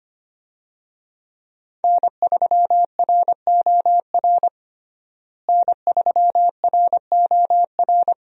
Morse code, heard as a keyed tone, about 25 wpm, 700 Hz. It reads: N3ROR N3ROR
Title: USA Call Signs - 5 Characters - Repeated 2x - 25wpm